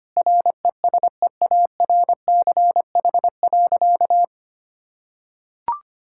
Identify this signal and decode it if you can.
Morse code, heard as a keyed tone, about 25 wpm, 700 Hz. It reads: RESEARCH.
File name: practice / 25wpm / Sentences with 9-Word Phrases - 25wpm